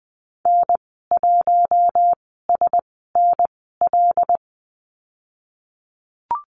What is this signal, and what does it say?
Morse code, heard as a keyed tone, about 20 wpm, spaced as for 15 wpm, 700 Hz. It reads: N1SNL
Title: 088 Lesson 15g - Number 5 - Call Signs 20wpm